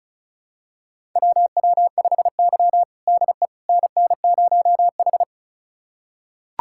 Morse code, heard as a keyed tone, about 36 wpm, 700 Hz. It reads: WW5Y DE NN0H
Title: USA Callsign DE Callsign - 4 Characters - 35wpm